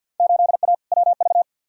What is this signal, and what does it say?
Morse code, heard as a keyed tone, about 52 wpm, 700 Hz. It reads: 8A WV
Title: Field Day Exchanges - 50wpm